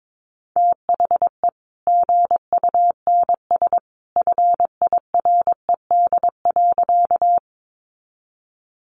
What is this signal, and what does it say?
Morse code, heard as a keyed tone, about 22 wpm, 700 Hz. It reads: THE GUNS FIRED.
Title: A Sound Of Thunder 22wpm